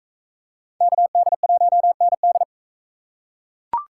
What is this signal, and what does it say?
Morse code, heard as a keyed tone, about 44 wpm, 700 Hz. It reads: KD1ND